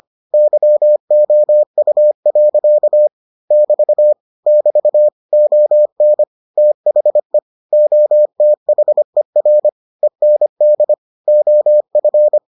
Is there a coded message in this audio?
Yes: Morse code from 0.3 to 12.4 s, heard as a keyed tone, about 25 wpm, 600 Hz. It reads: YOU. <BT> <BT> ON THE OTHER END OF